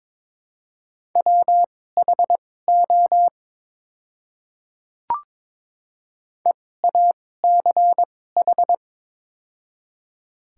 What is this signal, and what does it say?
Morse code, heard as a keyed tone, about 21 wpm, spaced as for 16 wpm, 700 Hz. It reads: WHO EACH
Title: 2x Character Spacing - Top 100 Words - 22wpm